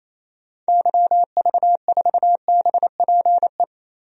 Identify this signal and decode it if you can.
Morse code, heard as a keyed tone, about 28 wpm, 700 Hz. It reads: YV4BPE
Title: DXCC-100 Call Signs - 6 Characters - 28wpm